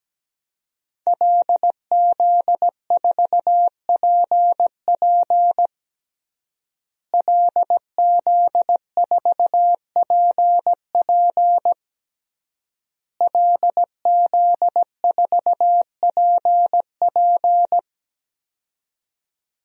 Morse code, heard as a keyed tone, about 17 wpm, 700 Hz. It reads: LZ4PP LZ4PP LZ4PP
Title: DXCC-100 Call Signs - 5 Characters - Repeated 3x - 17wpm